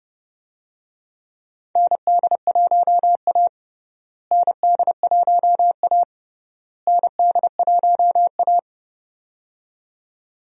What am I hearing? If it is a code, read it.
Morse code, heard as a keyed tone, about 32 wpm, 700 Hz. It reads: ND1A ND1A ND1A
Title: USA Call Signs - 4 Characters - Repeated 3x - 30wpm